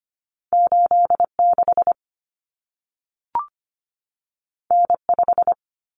Morse code, heard as a keyed tone, about 25 wpm, 700 Hz. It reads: 86 N5
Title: Common Cut Numbers - 2-Digit 25wpm